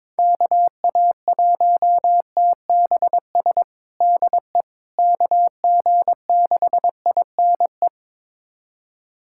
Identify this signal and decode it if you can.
Morse code, heard as a keyed tone, about 21 wpm, 700 Hz. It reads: KA1TBS DE KG6INE
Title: USA Callsign DE Callsign 22wpm